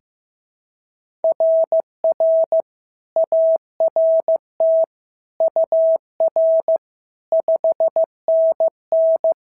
Morse code, heard as a keyed tone, about 14 wpm, 650 Hz. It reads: RR ART UR 5NN
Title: SOTA QSO Practice 15wpm